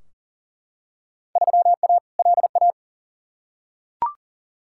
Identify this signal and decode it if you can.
Morse code, heard as a keyed tone, about 40 wpm, 700 Hz. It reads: 3A LA
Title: Field Day Exchanges - 40wpm